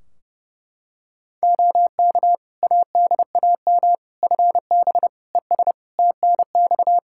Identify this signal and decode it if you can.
Morse code, heard as a keyed tone, about 30 wpm, 700 Hz. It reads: OK ADAM FB ES TNX
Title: Practice QSO - Exchange 2 - Info - 30wpm